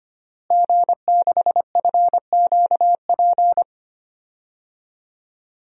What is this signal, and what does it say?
Morse code, heard as a keyed tone, about 25 wpm, 700 Hz. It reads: G6FQP
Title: DXCC-100 Call Signs - 5 Characters - Repeated 3x - 25wpm